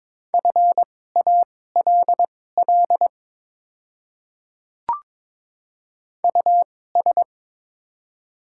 Morse code, heard as a keyed tone, about 22 wpm, spaced as for 16 wpm, 690 Hz. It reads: FALL US
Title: 2x Character Spacing - Top 500 Words - 22wpm